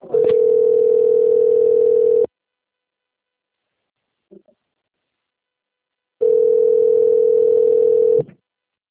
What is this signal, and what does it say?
Ringback tone (North America), 440 + 480 Hz.